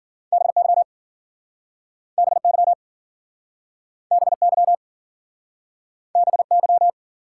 Morse code, heard as a keyed tone, about 52 wpm, 700 Hz. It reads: BY BY BY BY